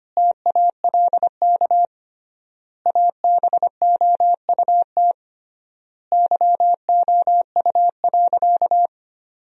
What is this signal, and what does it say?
Morse code, heard as a keyed tone, about 25 wpm, 700 Hz. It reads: TALK ABOUT YOU.
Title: 3x Word-Spacing - Sentences from Top 400 Words - 25wpm